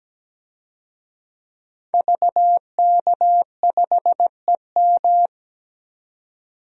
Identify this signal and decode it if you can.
Morse code, heard as a keyed tone, about 17 wpm, 700 Hz. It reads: VK5EM
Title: DXCC-100 Call Signs - 5 Characters - 17wpm